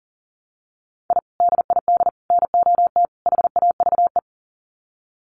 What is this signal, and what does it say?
Morse code, heard as a keyed tone, about 40 wpm, 700 Hz. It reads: I DID NOT HAVE
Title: Sets of 4 Words 40wpm